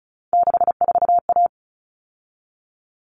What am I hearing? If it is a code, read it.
Morse code, heard as a keyed tone, about 34 wpm, 700 Hz. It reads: 64A
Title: All Cut Numbers - 3-Digit 35wpm